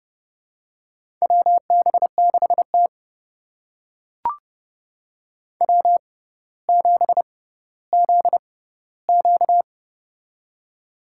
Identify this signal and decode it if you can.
Morse code, heard as a keyed tone, about 30 wpm, spaced as for 15 wpm, 700 Hz. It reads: WB6T W7ZQ
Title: USA Call Signs - 4 Characters - 6x Character Spacing - 30wpm